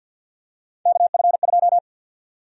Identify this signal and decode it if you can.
Morse code, heard as a keyed tone, about 52 wpm, 700 Hz. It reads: KF2
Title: USA Call Sign Prefixes - 50wpm